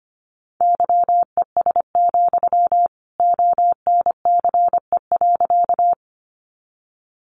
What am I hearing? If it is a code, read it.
Morse code, heard as a keyed tone, about 25 wpm, 700 Hz. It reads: YES, ONCE.
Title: Sentences from Top 500 Words 25wpm